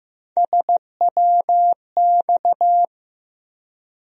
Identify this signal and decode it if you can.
Morse code, heard as a keyed tone, about 15 wpm, 700 Hz. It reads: SWX
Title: USA Call Sign Postfixes - 3 Letters 15wpm